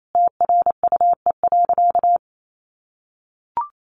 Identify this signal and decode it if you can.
Morse code, heard as a keyed tone, about 28 wpm, 700 Hz. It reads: TRUE.
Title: Sentences from Top 1300 Words 28wpm